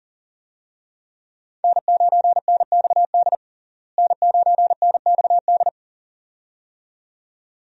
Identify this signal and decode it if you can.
Morse code, heard as a keyed tone, about 40 wpm, 700 Hz. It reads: N9NXD N9NXD
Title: USA Call Signs - 5 Characters - Repeated 2x - 40wpm